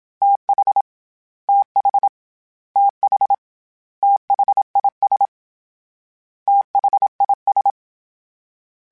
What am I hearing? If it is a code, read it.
Morse code, heard as a keyed tone, about 27 wpm, 800 Hz. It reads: TH TH TH THIS THIS
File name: THs / TH / TH Prefixes 3 @25